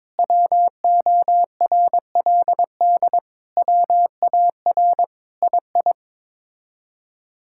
Morse code, heard as a keyed tone, about 22 wpm, 700 Hz. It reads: WORLD WAR II
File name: practice / 22wpm / Sets of 7 Words - 22wpm